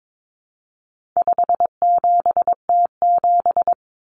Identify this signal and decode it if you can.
Morse code, heard as a keyed tone, about 22 wpm, 700 Hz. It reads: 57T7
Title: Common Cut Numbers - 4-Digit 22wpm